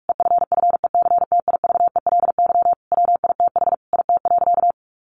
Morse code, heard as a keyed tone, about 44 wpm, 700 Hz. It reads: EFFECTIVELY WITH IT.
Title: Sentences from Top 1400 Words 45wpm